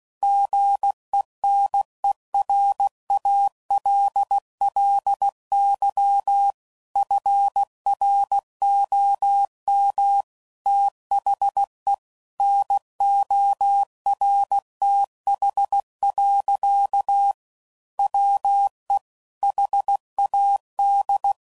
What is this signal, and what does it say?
Morse code, heard as a keyed tone, about 16 wpm, 800 Hz. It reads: GENERALLY FROM THE NORTH. WE HAD
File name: CW Code Groups @15WPM / Weather Sets / Weather 4 24